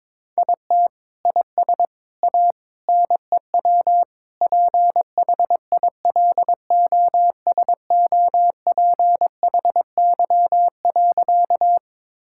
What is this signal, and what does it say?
Morse code, heard as a keyed tone, about 22 wpm, 700 Hz. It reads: IT IS A NEW PHILOSOPHY.
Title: Sentences from Top 1000 Words - Review Part 6 - 22wpm